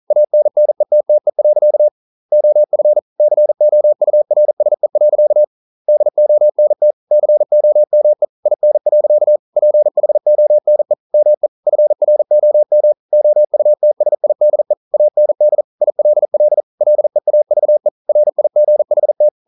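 Morse code, heard as a keyed tone, about 40 wpm, 600 Hz. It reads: ANNETTE. OF COURSE. DONT COME IN. PHONE ME FROM OUTSIDE AND ILL LEAVE RIGHT
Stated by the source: Morse code practice at 41 wpm, from CW med Gnister.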